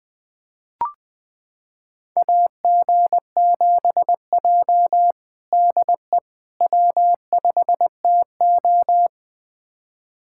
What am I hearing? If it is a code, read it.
Morse code, heard as a keyed tone, about 20 wpm, 700 Hz. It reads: AG7J DE W5TO
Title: USA Callsign DE Callsign - 4 Characters - 20wpm